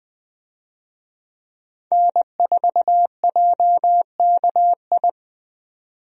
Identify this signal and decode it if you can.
Morse code, heard as a keyed tone, about 20 wpm, 700 Hz. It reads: N4JKI
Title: USA Call Signs - 5 Characters - 20wpm